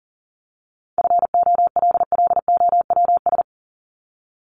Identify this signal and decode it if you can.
Morse code, heard as a keyed tone, about 40 wpm, 700 Hz. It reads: FOLLOWS
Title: ES:S Suffix - Top 100 Words with Base Words 40wpm